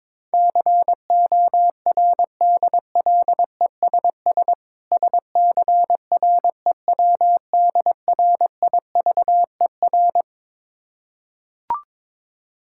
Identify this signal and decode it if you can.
Morse code, heard as a keyed tone, about 22 wpm, 690 Hz. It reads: CORDLESS SCREWDRIVER
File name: Difficult Words with Hints - 22wpm